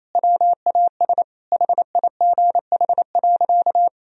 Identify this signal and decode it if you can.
Morse code, heard as a keyed tone, about 28 wpm, 700 Hz. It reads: WAS HIGH.
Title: Sentences from Top 1000 Words - Review Part 4 - 28wpm